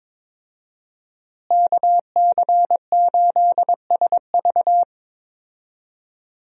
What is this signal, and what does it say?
Morse code, heard as a keyed tone, about 22 wpm, 700 Hz. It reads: KC8SV
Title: USA Call Signs - 5 Characters - Repeated 3x - 22wpm